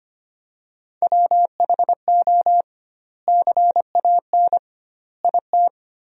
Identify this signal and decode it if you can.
Morse code, heard as a keyed tone, about 25 wpm, 700 Hz. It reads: WHO CAN IT
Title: 2x Word-Spacing - Sentences from Top 100 Words - 25wpm